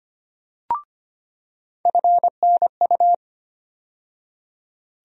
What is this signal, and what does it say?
Morse code, heard as a keyed tone, about 25 wpm, 700 Hz. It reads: FNU
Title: USA Call Sign Postfixes - 3 Letters 25wpm